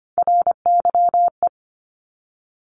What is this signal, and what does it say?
Morse code, heard as a keyed tone, about 25 wpm, 700 Hz. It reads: RYE
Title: Multi-letter Phonemes 25wpm